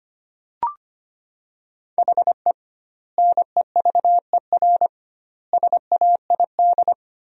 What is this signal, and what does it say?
Morse code, heard as a keyed tone, about 25 wpm, 700 Hz. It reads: HE NEVER SAID
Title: 2x Word-Spacing - Sentences from Top 500 Words - 25wpm